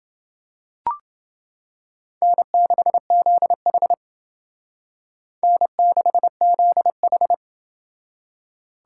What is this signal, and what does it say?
Morse code, heard as a keyed tone, about 28 wpm, 700 Hz. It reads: N6ZH N6ZH